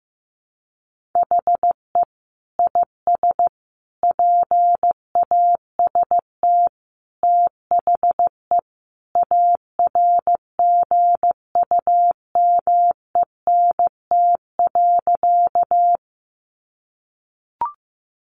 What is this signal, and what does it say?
Morse code, heard as a keyed tone, about 14 wpm, 700 Hz. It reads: HE IS PAST THE ARGUMENT.
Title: Sentences from Top 1400 Words 15wpm